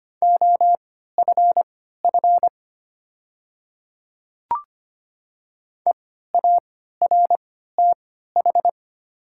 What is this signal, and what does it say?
Morse code, heard as a keyed tone, about 25 wpm, spaced as for 14 wpm, 700 Hz. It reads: OFF EARTH
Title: 3x Character Spacing - Top 500 Words - 25wpm